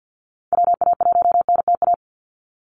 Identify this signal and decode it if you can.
Morse code, heard as a keyed tone, about 49 wpm, 700 Hz. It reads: WA1NTA
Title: Call Signs Repeated 3x - 50wpm